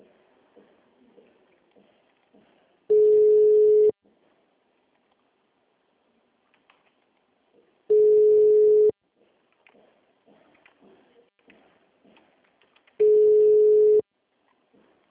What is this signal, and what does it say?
Ringback tone (Europe, 425 Hz).